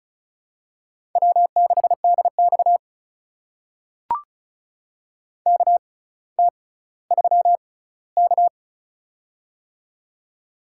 Morse code, heard as a keyed tone, about 36 wpm, spaced as for 18 wpm, 700 Hz. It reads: W6DX KT3K
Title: USA Call Signs - 4 Characters - 6x Character Spacing - 35wpm